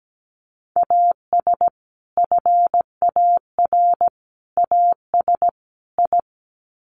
Morse code, heard as a keyed tone, about 17 wpm, 700 Hz. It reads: AS FAR AS I
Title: Sets of 4 Words 17wpm